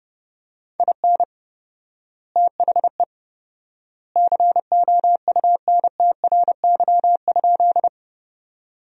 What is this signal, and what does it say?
Morse code, heard as a keyed tone, about 32 wpm, 700 Hz. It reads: IN THE COUNTRY?
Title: 4x Word-Spacing - Sentences from Top 300 Words - 30wpm